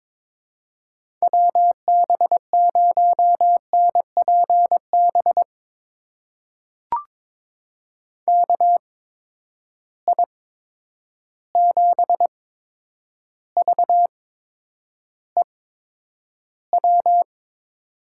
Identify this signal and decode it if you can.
Morse code, heard as a keyed tone, about 22 wpm, spaced as for 9 wpm, 700 Hz. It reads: WB0NPB KI7VEW
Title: USA Call Signs - 6 Characters - 8x Character Spacing - 22wpm